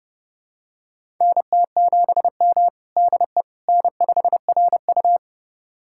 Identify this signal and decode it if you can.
Morse code, heard as a keyed tone, about 32 wpm, 700 Hz. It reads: NT7M DE N5RU
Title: USA Callsign DE Callsign - 4 Characters - 30wpm